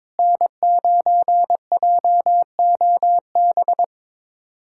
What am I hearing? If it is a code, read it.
Morse code, heard as a keyed tone, about 22 wpm, 700 Hz. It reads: N9JOB
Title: USA Callsign DE Callsign - 5 Characters - 22wpm